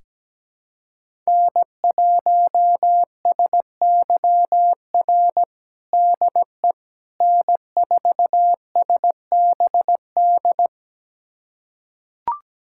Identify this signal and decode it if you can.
Morse code, heard as a keyed tone, about 17 wpm, 700 Hz. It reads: N1SYR DE N4SBD
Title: USA Callsign DE Callsign - 5 Characters - 17wpm